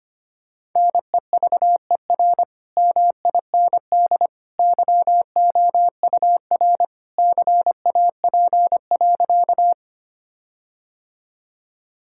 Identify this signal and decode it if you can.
Morse code, heard as a keyed tone, about 25 wpm, 700 Hz. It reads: NEVER MIND YOUR CAP.
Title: Sentences from Top 1600 Words 25wpm